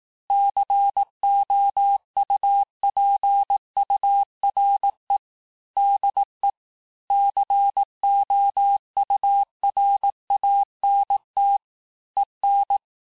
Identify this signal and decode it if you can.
Morse code, heard as a keyed tone, about 18 wpm, 800 Hz. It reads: COUPURE DE COURANT EN
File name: cours-du-lundi files / cours-1594--FIN-de-QSO-vitesse-17mpm